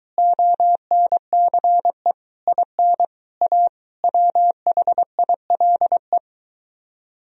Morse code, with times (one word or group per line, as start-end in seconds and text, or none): 0.2-2.1 ONCE
2.5-3.0 IN
3.4-3.7 A
4.0-6.2 WHILE